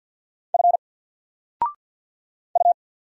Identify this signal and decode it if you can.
Morse code, heard as a keyed tone, about 50 wpm, 700 Hz. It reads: F U
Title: Single Character (Letter:Number:Punctuation:Prosign) 50wpm